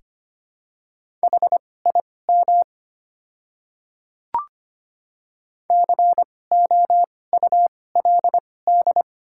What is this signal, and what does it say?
Morse code, heard as a keyed tone, about 25 wpm, spaced as for 18 wpm, 690 Hz. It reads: HIM COULD
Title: 2x Character Spacing - Top 100 Words - 25wpm